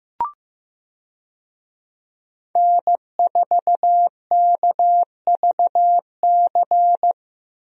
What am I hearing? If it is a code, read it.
Morse code, heard as a keyed tone, about 16 wpm, 700 Hz. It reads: N4KVC
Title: USA Call Signs - 5 Characters - Repeated 3x - 15wpm